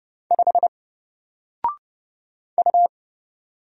Morse code, heard as a keyed tone, about 30 wpm, 700 Hz. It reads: 5 U
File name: Single-Letter-Number-30wpm